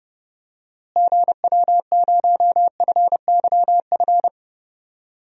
Morse code, heard as a keyed tone, about 29 wpm, 700 Hz. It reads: GW0FYF
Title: DXCC-100 Call Signs - 6 Characters - Repeated 3x - 30wpm